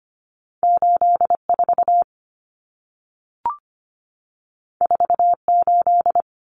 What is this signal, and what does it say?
Morse code, heard as a keyed tone, about 25 wpm, 700 Hz. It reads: 84 48
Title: Common Cut Numbers - 2-Digit 25wpm